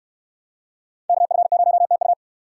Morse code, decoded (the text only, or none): KF0TU